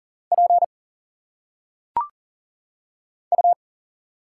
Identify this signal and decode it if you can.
Morse code, heard as a keyed tone, about 41 wpm, 700 Hz. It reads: P U
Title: Single Letters - Phonetic Pronunciation 40wpm